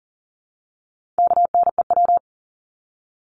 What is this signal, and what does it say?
Morse code, heard as a keyed tone, about 40 wpm, 700 Hz. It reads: KNEW
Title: Top 500 Words 40wpm